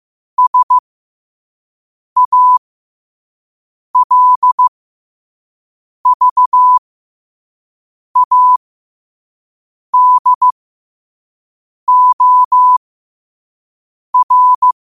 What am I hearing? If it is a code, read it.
Morse code, heard as a keyed tone, about 15 wpm, spaced as for 5 wpm, 1000 Hz. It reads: SALVADOR
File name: qotdfiles / 5wpm